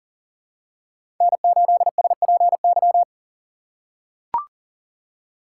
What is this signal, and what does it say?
Morse code, heard as a keyed tone, about 40 wpm, 700 Hz. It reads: N8SPY
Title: USA Call Signs - 5 Characters - 4x Character Spacing - 40wpm